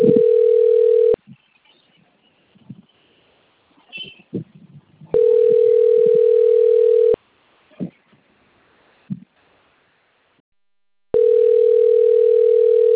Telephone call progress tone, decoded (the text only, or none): ringback tone